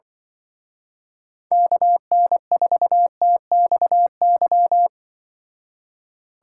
Morse code, heard as a keyed tone, about 25 wpm, 700 Hz. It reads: KN4TXY